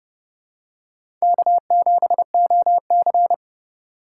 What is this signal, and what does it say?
Morse code, heard as a keyed tone, about 30 wpm, 690 Hz. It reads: K7OC
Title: USA Call Signs - 4 Characters - 2x Character Spacing - 30wpm